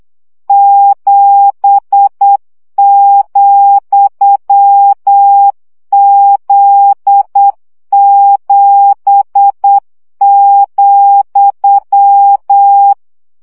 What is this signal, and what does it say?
Morse code, heard as a keyed tone, about 8 wpm, 800 Hz. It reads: 7,Z7,
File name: CW gyakorlas z 7 vesszo 7 baud